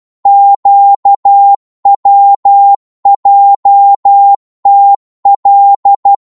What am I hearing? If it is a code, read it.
Morse code, heard as a keyed tone, about 12 wpm, 800 Hz. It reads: QWJTL
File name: Morse / ltr-12wpm